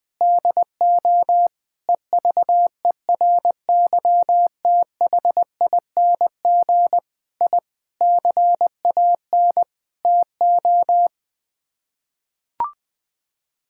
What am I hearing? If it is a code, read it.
Morse code, heard as a keyed tone, about 20 wpm, 700 Hz. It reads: DO EVERYTHING I CAN TO
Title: Sets of 8 Words - 20wpm